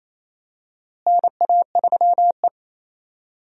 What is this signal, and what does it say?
Morse code, heard as a keyed tone, about 27 wpm, 700 Hz. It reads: NA3E
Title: USA Call Signs - 4 Characters - Repeated 3x - 28wpm